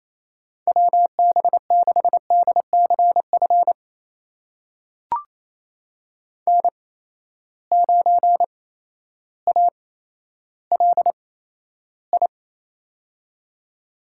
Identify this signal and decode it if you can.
Morse code, heard as a keyed tone, about 28 wpm, spaced as for 13 wpm, 700 Hz. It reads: WB6DCF N9ALI
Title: USA Call Signs - 8x Character Spacing - 28wpm